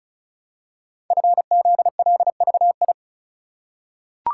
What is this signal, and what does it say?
Morse code, heard as a keyed tone, about 36 wpm, 700 Hz. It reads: FZLVI
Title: Code Groups - 5 Letters - 35wpm